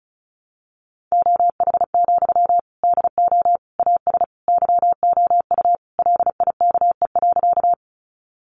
Morse code, heard as a keyed tone, about 34 wpm, 700 Hz. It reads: OH, DO AS YOU LIKE.
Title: Sentences from Top 500 Words 35wpm